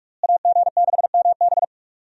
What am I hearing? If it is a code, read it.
Morse code, heard as a keyed tone, about 45 wpm, 700 Hz. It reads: AG6MB
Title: USA Call Signs - 5 Characters - Repeated 2x - 45wpm